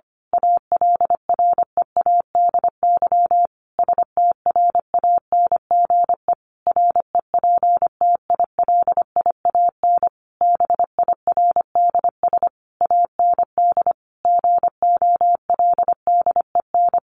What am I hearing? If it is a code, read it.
Morse code, heard as a keyed tone, about 25 wpm, 700 Hz. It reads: ALREADY STRANGE REPTILIAN BIRDS AND GOLDEN
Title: A Sound Of Thunder 25wpm